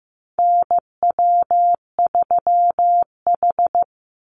Morse code, heard as a keyed tone, about 15 wpm, 700 Hz. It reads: NW3H
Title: USA Call Signs 15wpm